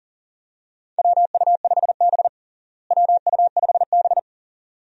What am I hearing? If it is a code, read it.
Morse code, heard as a keyed tone, about 40 wpm, 700 Hz. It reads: WU5B WU5B
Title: USA Call Signs - 4 Characters - Repeated 2x - 40wpm